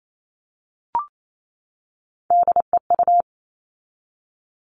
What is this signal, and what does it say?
Morse code, heard as a keyed tone, about 28 wpm, 700 Hz. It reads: DEU
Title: All Cut Numbers - 3-Digit 28wpm